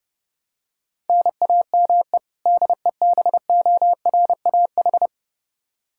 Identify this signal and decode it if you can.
Morse code, heard as a keyed tone, about 30 wpm, 700 Hz. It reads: NAME DEBORAH
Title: Practice QSO - Exchange 1 - Beginning - 30wpm